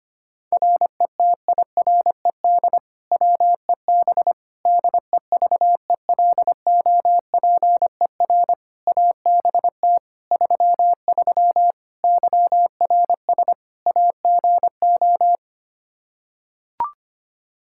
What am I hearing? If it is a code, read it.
Morse code, heard as a keyed tone, about 25 wpm, 700 Hz. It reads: RETIRED WEB DEVELOPER ABT 33 YRS AGO
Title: K1USN SST QSO Practice 25wpm